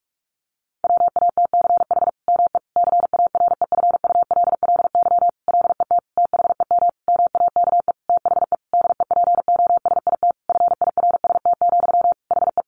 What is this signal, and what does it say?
Morse code, heard as a keyed tone, about 44 wpm, 700 Hz. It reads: WATCH ME CAREFULLY LET THEM MAKE THE DEPOSIT FIRST, HE
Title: Sentences from Top 1300 Words 45wpm